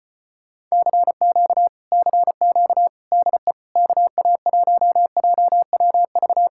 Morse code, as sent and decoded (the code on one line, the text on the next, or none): -.-. --.- / -.-. --.- / -.. / -.- .- .---- .--- .-- ...-
CQ CQ DE KA1JWV